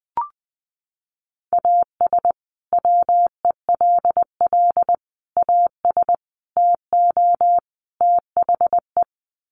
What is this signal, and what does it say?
Morse code, heard as a keyed tone, about 19 wpm, 700 Hz. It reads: AS WELL AS TO THE
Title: Sets of 5 Words Encore2 20wpm